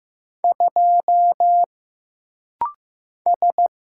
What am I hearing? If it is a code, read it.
Morse code, heard as a keyed tone, about 15 wpm, 700 Hz. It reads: 2 S
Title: Single Letter-Number Alt - Rapid-Fire 15wpm